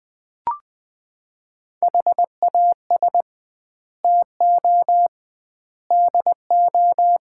Morse code, heard as a keyed tone, about 20 wpm, 700 Hz. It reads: HAS TO DO
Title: 2x Word-Spacing - Sets of 5 Words - 20wpm